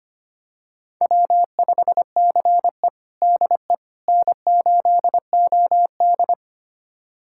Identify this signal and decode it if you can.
Morse code, heard as a keyed tone, about 25 wpm, 700 Hz. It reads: W5CE DE N8OD
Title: USA Callsign DE Callsign - 4 Characters - 25wpm